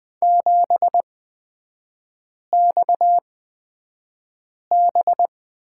Morse code, heard as a keyed tone, about 20 wpm, spaced as for 5 wpm, 690 Hz. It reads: 7XB
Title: 229 Lesson 37e - Letter x - Three Characters 20wpm